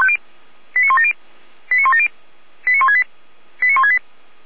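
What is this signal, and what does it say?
Five-tone selective call (ZVEI), heard as quick runs of five tones, tones of about 70 ms. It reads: 89170 89170 89168 89168
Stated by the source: Five-tone ZVEI selective call received by SDR.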